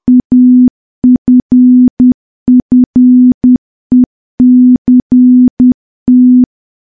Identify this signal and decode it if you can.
Morse code, heard as a keyed tone, about 10 wpm, 250 Hz. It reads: AFFECT
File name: morse / morse-training-10WPM